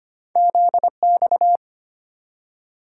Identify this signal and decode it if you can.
Morse code, heard as a keyed tone, about 25 wpm, 700 Hz. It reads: Z X
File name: Two Letter-Number Combinations - 25wpm